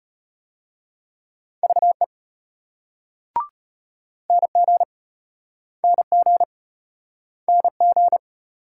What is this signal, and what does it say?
Morse code, heard as a keyed tone, about 33 wpm, 700 Hz. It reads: VE NG NG NG